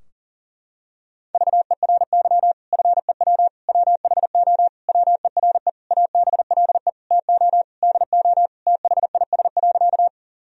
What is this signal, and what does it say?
Morse code, heard as a keyed tone, about 40 wpm, 700 Hz. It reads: VERY FEW WHO WERE ABLE TO DO THIS.
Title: Sentences from Top 1000 Words - Review Part 2 - 40wpm